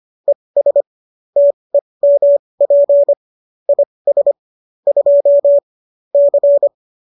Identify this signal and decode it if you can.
Morse code, heard as a keyed tone, about 25 wpm, spaced as for 20 wpm, 570 Hz. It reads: ES TEMP IS 2 C